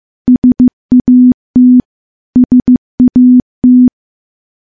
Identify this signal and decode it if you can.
Morse code, heard as a keyed tone, about 15 wpm, 250 Hz. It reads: SAT SAT
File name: morse / morse-training-15WPM